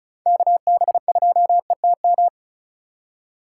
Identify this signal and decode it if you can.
Morse code, heard as a keyed tone, about 35 wpm, 690 Hz. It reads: KB2ETM
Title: USA Call Signs - 6 Characters - 6x Character Spacing - 35wpm